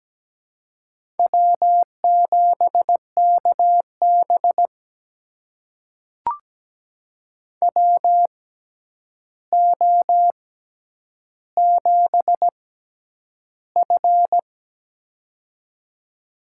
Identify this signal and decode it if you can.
Morse code, heard as a keyed tone, about 17 wpm, spaced as for 9 wpm, 700 Hz. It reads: W7KB WO7F
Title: USA Call Signs - 4 Characters - 6x Character Spacing - 17wpm